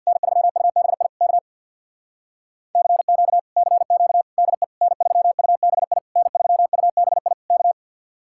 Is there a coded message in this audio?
Yes: Morse code from 0.1 to 7.7 s, heard as a keyed tone, about 64 wpm, 700 Hz. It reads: N3UBI K CQ CQ DE N3UBI N3UBI K